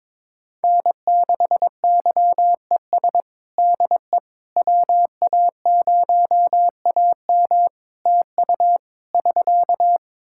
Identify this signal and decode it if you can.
Morse code, heard as a keyed tone, about 21 wpm, 700 Hz. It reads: N6YES DE WA0AM TU <SK>
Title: Practice QSO - Exchange 4 - Ending - 22wpm